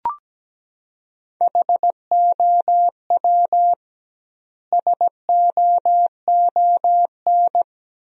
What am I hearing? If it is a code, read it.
Morse code, heard as a keyed tone, about 17 wpm, 700 Hz. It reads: HOW SOON
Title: 2x Word-Spacing - Sentences from Top 400 Words - 17wpm